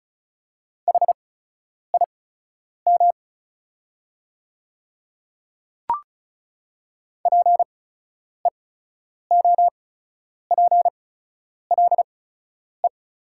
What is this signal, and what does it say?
Morse code, heard as a keyed tone, about 34 wpm, spaced as for 10 wpm, 700 Hz. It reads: HIM PEOPLE
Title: 8x Character Spacing - Top 100 Words - 35wpm